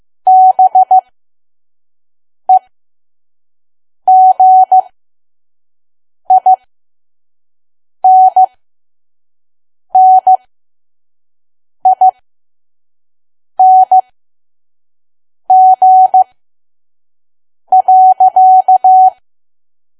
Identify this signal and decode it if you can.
Morse code, heard as a keyed tone, about 15 wpm, spaced as for 5 wpm, 750 Hz. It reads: BEGINNING.
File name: Archive / 5 WPM files / PM